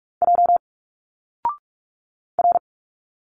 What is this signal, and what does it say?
Morse code, heard as a keyed tone, about 46 wpm, 700 Hz. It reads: J R
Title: Single Letters 45wpm